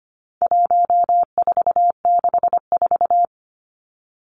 Morse code, heard as a keyed tone, about 25 wpm, 700 Hz. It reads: 1464
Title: Numbers - 4-Digit 25wpm